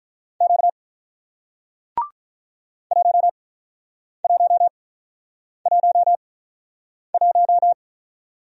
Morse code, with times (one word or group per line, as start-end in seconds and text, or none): 0.4-0.7 Q
2.9-7.7 1111